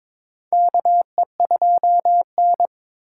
Morse code, heard as a keyed tone, about 22 wpm, 700 Hz. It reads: KE2N
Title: USA Call Signs - 4 Characters - 2x Character Spacing - 22wpm